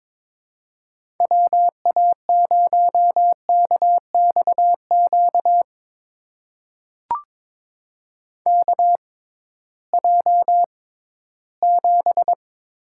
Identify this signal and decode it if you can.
Morse code, heard as a keyed tone, about 22 wpm, spaced as for 14 wpm, 700 Hz. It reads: WA0KXQ KJ7
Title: USA Call Signs - 6 Characters - 6x Character Spacing - 22wpm